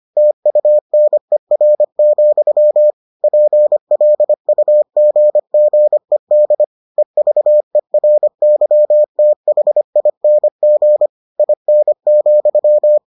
Morse code, heard as a keyed tone, about 25 wpm, 600 Hz. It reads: TUNER, PLUGGED EVERYTHING IN,